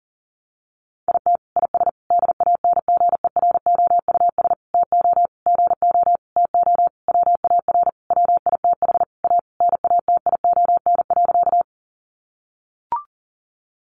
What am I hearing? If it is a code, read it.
Morse code, heard as a keyed tone, about 40 wpm, 700 Hz. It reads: IT IS DANGEROUS TO GO TO WAR WITH A NATION.
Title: Sentences from Top 1200 Words 40wpm